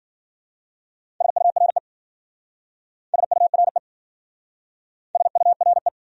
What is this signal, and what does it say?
Morse code, heard as a keyed tone, about 50 wpm, 700 Hz. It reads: SURE SURE SURE
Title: Speed-Racing Top 500 Words 40wpm